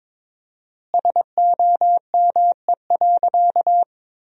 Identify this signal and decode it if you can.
Morse code, heard as a keyed tone, about 21 wpm, 700 Hz. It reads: SOME.
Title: 4x Word-Spacing - Sentences from Top 100 Words - 22wpm